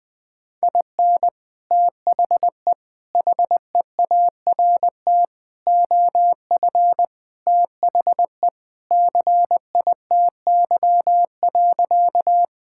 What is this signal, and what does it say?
Morse code, heard as a keyed tone, about 20 wpm, 700 Hz. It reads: IN THE HEART OF THE CITY.
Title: Sentences with 3-Word Phrases - 20wpm